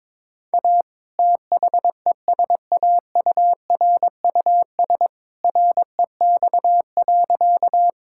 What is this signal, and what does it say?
Morse code, heard as a keyed tone, about 22 wpm, 700 Hz. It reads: A THESAURUS REX.
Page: morse-code-ninja-practice-22wpm